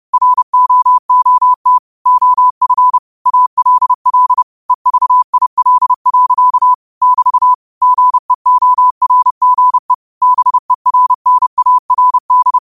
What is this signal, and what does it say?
Morse code, heard as a keyed tone, about 30 wpm, 1000 Hz. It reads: ROOT OF ALL EVIL. <BT> GEORGE BERNARD
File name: qotdfiles / 30wpm